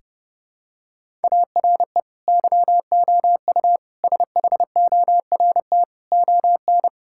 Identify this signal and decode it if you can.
Morse code, heard as a keyed tone, about 30 wpm, 700 Hz. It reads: ARE YOU SHORT ON
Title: Sentences from Top 1000 Words - Review Part 5 - 30wpm